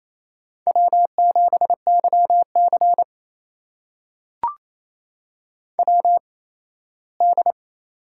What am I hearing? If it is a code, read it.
Morse code, heard as a keyed tone, about 28 wpm, spaced as for 17 wpm, 700 Hz. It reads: W7YC WD
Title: USA Call Signs - 4 Characters - 8x Character Spacing - 28wpm